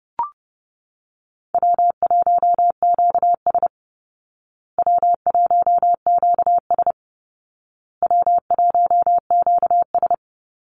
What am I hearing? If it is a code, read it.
Morse code, heard as a keyed tone, about 30 wpm, 700 Hz. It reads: W1QS W1QS W1QS
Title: Easy Call Signs Repeated 3x - 30wpm